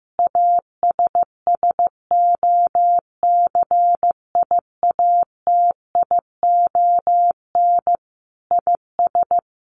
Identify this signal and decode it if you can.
Morse code, heard as a keyed tone, about 15 wpm, 700 Hz. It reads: ASSOCIATION IS
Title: Sentences from Top 1100 Words 15wpm